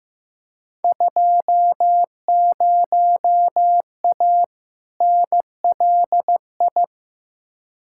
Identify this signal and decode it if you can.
Morse code, heard as a keyed tone, about 16 wpm, 700 Hz. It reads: 20A NLI
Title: Field Day Exchanges Encore - 15wpm